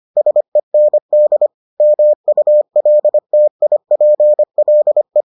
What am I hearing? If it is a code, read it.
Morse code, heard as a keyed tone, about 25 wpm, 600 Hz. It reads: SEND MULTIPLE